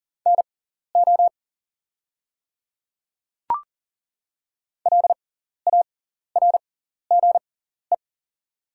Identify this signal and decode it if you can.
Morse code, heard as a keyed tone, about 40 wpm, spaced as for 14 wpm, 690 Hz. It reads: NO LARGE
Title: 6x Character Spacing - Top 200 Words - 40wpm